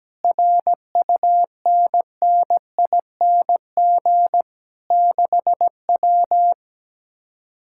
Morse code, heard as a keyed tone, about 17 wpm, 700 Hz. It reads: RUNNING 6W
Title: Speed-Racing Common QSO Phrases Encore - 17wpm